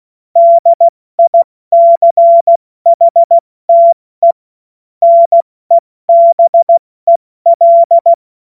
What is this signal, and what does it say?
Morse code, heard as a keyed tone, about 16 wpm, 680 Hz. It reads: DICHTE NEBEL
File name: Morsekurs / Witze / Witz 16. August 70 F80